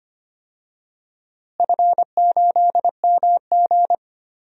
Morse code, heard as a keyed tone, about 25 wpm, 700 Hz. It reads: F8MG
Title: DXCC-100 Call Signs - 4 Characters - Repeated 2x - 25wpm